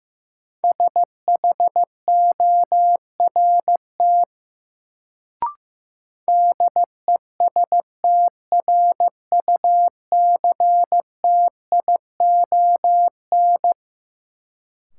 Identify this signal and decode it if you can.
Morse code, heard as a keyed tone, about 15 wpm, 700 Hz. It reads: SHORT DESTRUCTION
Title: Sets of 5 Words Encore - New Word Drill - 15wpm